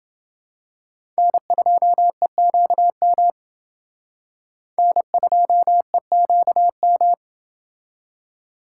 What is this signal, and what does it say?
Morse code, heard as a keyed tone, about 28 wpm, 700 Hz. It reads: N2EQM N2EQM